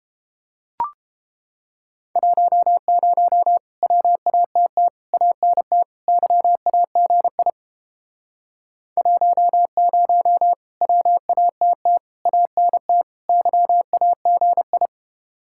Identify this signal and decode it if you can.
Morse code, heard as a keyed tone, about 29 wpm, 700 Hz. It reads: 10 WATT ANT YAGI 10 WATT ANT YAGI